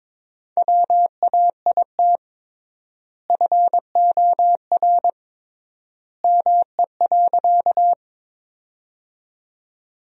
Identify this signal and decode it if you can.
Morse code, heard as a keyed tone, about 21 wpm, 700 Hz. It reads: WAIT FOR ME.
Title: 3x Word-Spacing - Sentences from Top 500 Words - 22wpm